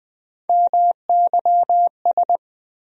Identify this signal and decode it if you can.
Morse code, heard as a keyed tone, about 20 wpm, 700 Hz. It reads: MYS
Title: USA Call Sign Postfixes - 3 Letters 20wpm